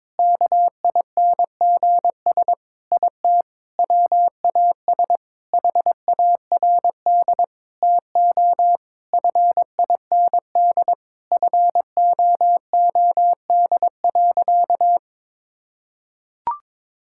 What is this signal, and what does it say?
Morse code, heard as a keyed tone, about 22 wpm, 700 Hz. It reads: KINGS IT WAS HARD TO FIND FOOD.
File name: practice / 22wpm / Sentences with 3-Word Phrases - 22wpm